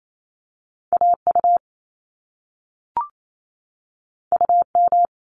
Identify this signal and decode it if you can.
Morse code, heard as a keyed tone, about 28 wpm, 700 Hz. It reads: AU UM
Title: Top Two Letter Combinations in Words 28wpm